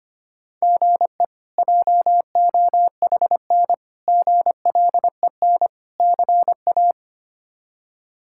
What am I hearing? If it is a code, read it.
Morse code, heard as a keyed tone, about 25 wpm, 700 Hz. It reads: GE JOHN GLEN CA
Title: K1USN SST QSO Practice - 25wpm